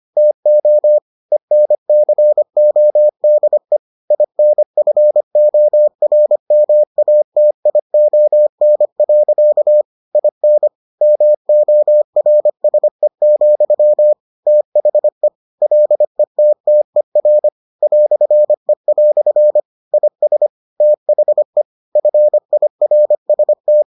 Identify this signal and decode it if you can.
Morse code, heard as a keyed tone, about 25 wpm, 600 Hz. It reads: TO ENCODE INFORMATION. IN MORSE, THE LETTER "E" IS THE FIRST